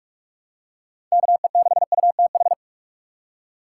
Morse code, heard as a keyed tone, about 46 wpm, 700 Hz. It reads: KE6UTH
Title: USA Call Signs - 6 Characters - 3x Character Spacing - 45wpm